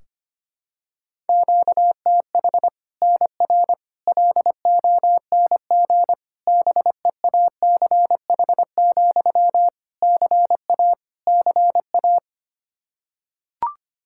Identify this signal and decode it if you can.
Morse code, heard as a keyed tone, about 25 wpm, 700 Hz. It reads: QTH NR LONG BEACH, CA CA
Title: K1USN SST QSO Practice 25wpm